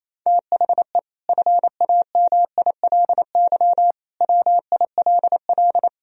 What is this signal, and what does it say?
Morse code, heard as a keyed tone, about 27 wpm, 700 Hz. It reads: THE FAMILY WILL
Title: Sentences from Top 1000 Words - Review Part 2 - 28wpm